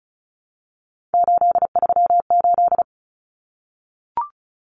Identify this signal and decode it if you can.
Morse code, heard as a keyed tone, about 34 wpm, 700 Hz. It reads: 838
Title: Numbers - 3-Digit 35wpm